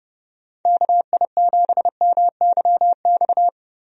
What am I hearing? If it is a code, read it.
Morse code, heard as a keyed tone, about 30 wpm, 700 Hz. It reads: KI7MYX
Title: USA Call Signs - 6 Characters - 30wpm